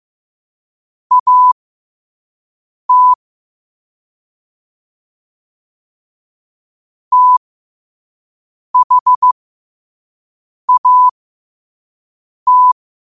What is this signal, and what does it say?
Morse code, heard as a keyed tone, about 15 wpm, spaced as for 5 wpm, 1000 Hz. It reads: AT THAT
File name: qotdfiles / 5wpm